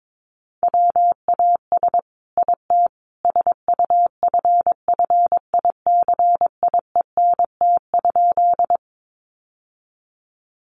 Morse code, heard as a keyed tone, about 22 wpm, 700 Hz. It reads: WAS IT SUFFICIENT?
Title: Sentences from Top 1400 Words 22wpm